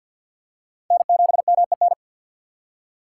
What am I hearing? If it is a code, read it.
Morse code, heard as a keyed tone, about 50 wpm, 700 Hz. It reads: N7MEN
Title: USA Call Signs - 4x Character Spacing - 50wpm